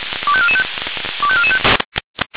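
Five-tone selective call (ZVEI), heard as quick runs of five tones, tones of about 70 ms. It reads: 25445 25445
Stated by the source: ZVEI five-tone selective call received on SDR.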